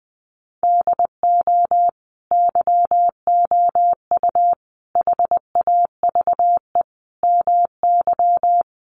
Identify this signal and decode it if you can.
Morse code, heard as a keyed tone, about 20 wpm, 700 Hz. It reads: DO YOU HAVE MY
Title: Sentences from Top 1400 Words 20wpm